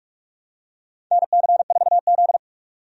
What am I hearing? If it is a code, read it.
Morse code, heard as a keyed tone, about 44 wpm, 700 Hz. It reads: NC4Z
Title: USA Call Signs - 4 Characters - 45wpm